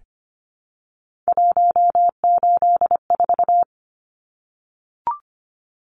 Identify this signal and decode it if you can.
Morse code, heard as a keyed tone, about 25 wpm, 700 Hz. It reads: 184
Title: Numbers - 3-Digit 25wpm